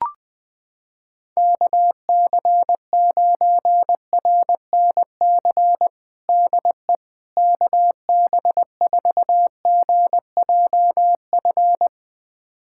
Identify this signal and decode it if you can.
Morse code, heard as a keyed tone, about 20 wpm, 700 Hz. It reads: KC9RNC DE KB4GJF
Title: USA Callsign DE Callsign 20wpm